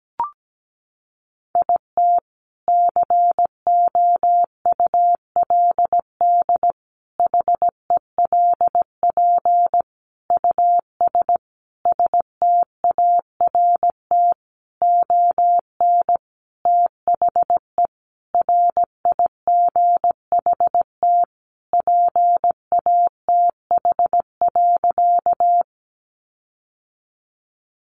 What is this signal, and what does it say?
Morse code, heard as a keyed tone, about 17 wpm, 700 Hz. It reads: IT COULD HELP US START ON THE RIGHT PATH.
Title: Sentences from Top 1400 Words 17wpm